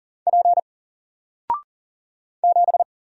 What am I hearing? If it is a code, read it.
Morse code, heard as a keyed tone, about 40 wpm, 700 Hz. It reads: P 7
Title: Single Letter-Number 40wpm